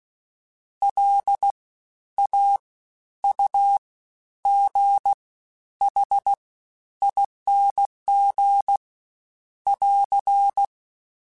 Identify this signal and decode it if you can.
Morse code, heard as a keyed tone, about 16 wpm, 800 Hz. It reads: L A U G H ING <AR>
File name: Special INGs-mp3s / Special INGs-mp3s / SlowToFaster ing @15